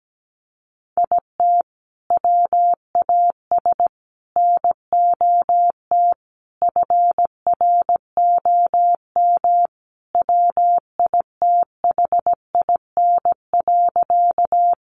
Morse code, heard as a keyed tone, about 17 wpm, 700 Hz. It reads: IT WAS NOT FROM WITHIN.
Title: Sentences from Top 700 Words 17wpm